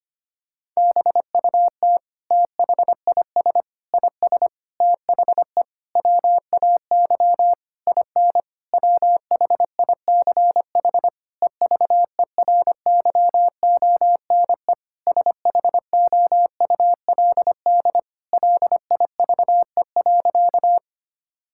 Morse code, heard as a keyed tone, about 25 wpm, 700 Hz. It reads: BUT THIS IS THE WAY IN WHICH EVERYONE SHOULD LIVE.
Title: Sentences from Top 1000 Words - Review Part 5 - 25wpm